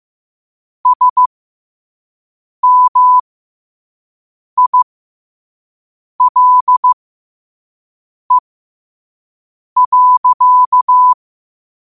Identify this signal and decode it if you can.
Morse code, heard as a keyed tone, about 15 wpm, spaced as for 5 wpm, 1000 Hz. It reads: SMILE.
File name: qotdfiles / 5wpm